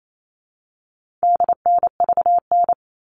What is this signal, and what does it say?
Morse code, heard as a keyed tone, about 28 wpm, 700 Hz. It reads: DNVN
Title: All Cut Numbers - 4-Digit 28wpm